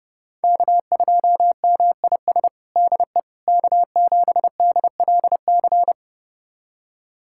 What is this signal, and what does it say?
Morse code, heard as a keyed tone, about 30 wpm, 700 Hz. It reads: K2MIS DE K7DLC
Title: USA Callsign DE Callsign - 5 Characters - 30wpm